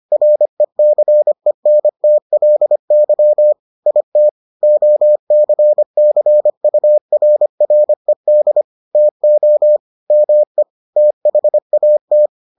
Morse code, heard as a keyed tone, about 25 wpm, 600 Hz. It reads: RECENTLY IT OCCURRED TO ME THAT